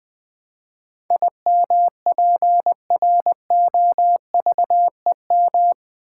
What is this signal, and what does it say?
Morse code, heard as a keyed tone, about 20 wpm, 700 Hz. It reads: IMPROVEM
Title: Build A Word - 4 Letters At A Time - 11-Letter Words - 20wpm